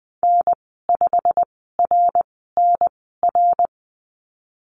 Morse code, heard as a keyed tone, about 20 wpm, spaced as for 15 wpm, 700 Hz. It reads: N5RNR
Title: 122 Lesson 20g - Number 3 - Call Signs 20wpm